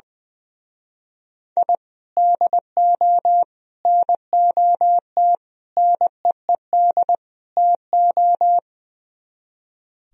Morse code, heard as a keyed tone, about 20 wpm, 700 Hz. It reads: I DO NOT NEED TO
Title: Speed-Racing - Sets of 5 Words 20wpm